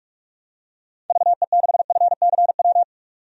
Morse code, heard as a keyed tone, about 46 wpm, 700 Hz. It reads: VE6FCW
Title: DXCC-100 Call Signs - 6 Characters - 45wpm